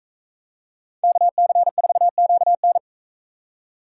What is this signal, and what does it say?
Morse code, heard as a keyed tone, about 41 wpm, 700 Hz. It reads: KC4QN